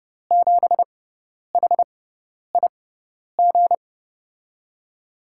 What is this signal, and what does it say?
Morse code, heard as a keyed tone, about 30 wpm, spaced as for 10 wpm, 700 Hz. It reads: 7HIG
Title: USA Call Signs - 6 Characters - 6x Character Spacing - 30wpm